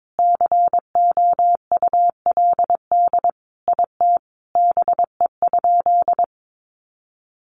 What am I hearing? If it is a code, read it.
Morse code, heard as a keyed tone, about 22 wpm, 700 Hz. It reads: COULD IT BE?
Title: Sentences from Top 1200 Words 22wpm